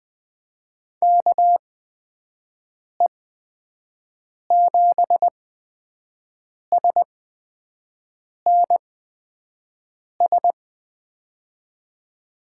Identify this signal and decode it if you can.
Morse code, heard as a keyed tone, about 21 wpm, spaced as for 5 wpm, 700 Hz. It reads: KE7SNS
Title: USA Call Signs - 6 Characters - 8x Character Spacing - 20wpm